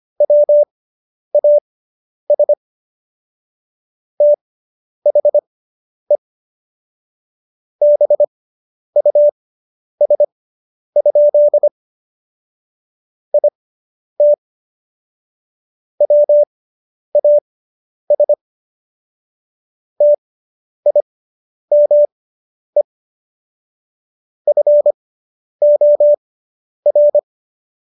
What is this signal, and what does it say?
Morse code, heard as a keyed tone, about 25 wpm, spaced as for 10 wpm, 590 Hz. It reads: WAS THE BUS? IT WAS TIME FOR